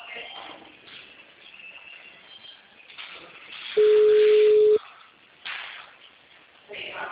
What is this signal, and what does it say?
Ringback tone (Europe, 425 Hz), with speech.